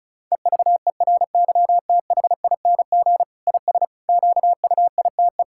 Morse code, heard as a keyed tone, about 36 wpm, 700 Hz. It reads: EVERYTHING IS QUITE
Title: Sentences from Top 1000 Words - Review Part 6 - 35wpm